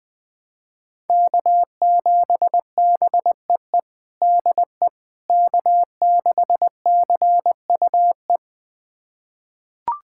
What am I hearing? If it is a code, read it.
Morse code, heard as a keyed tone, about 20 wpm, 700 Hz. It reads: K7BEE DE K6CUE
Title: USA Callsign DE Callsign - 5 Characters - 20wpm